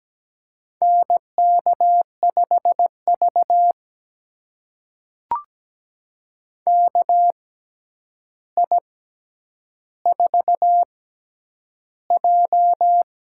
Morse code, heard as a keyed tone, about 17 wpm, spaced as for 9 wpm, 700 Hz. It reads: NK5V KI4J
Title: USA Call Signs - 4 Characters - 6x Character Spacing - 17wpm